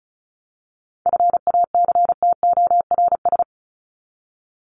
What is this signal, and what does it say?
Morse code, heard as a keyed tone, about 34 wpm, 700 Hz. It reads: FACTORS
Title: ES:S Suffix - Top 100 Words with Base Words 35wpm